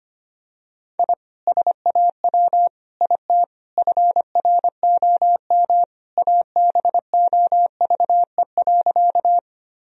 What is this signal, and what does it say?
Morse code, heard as a keyed tone, about 25 wpm, 700 Hz. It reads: I SAW IT FROM ABOVE.
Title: Sentences from Top 1000 Words - Review Part 2 - 25wpm